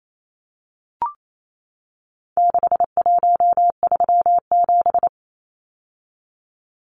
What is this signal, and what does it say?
Morse code, heard as a keyed tone, about 28 wpm, 700 Hz. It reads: 6137
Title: Numbers - 4-Digit 28wpm